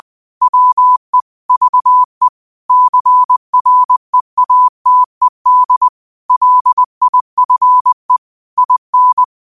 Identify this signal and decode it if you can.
Morse code, heard as a keyed tone, about 20 wpm, 1000 Hz. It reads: WE VE CREATED LIFE IN
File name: qotdfiles / 20wpm